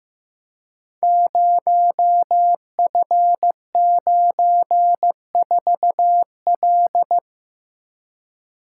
Morse code, heard as a keyed tone, about 15 wpm, 700 Hz. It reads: 0F94L
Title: Code Groups - 5 Letter:Numbers - 15wpm